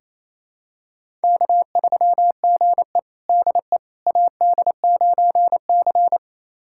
Morse code, heard as a keyed tone, about 28 wpm, 700 Hz. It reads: K3GE DE AD9C
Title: USA Callsign DE Callsign - 4 Characters - 28wpm